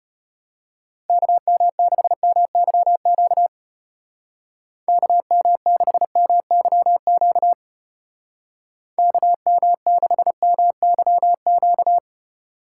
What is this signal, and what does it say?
Morse code, heard as a keyed tone, about 31 wpm, 700 Hz. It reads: KM6MYQ KM6MYQ KM6MYQ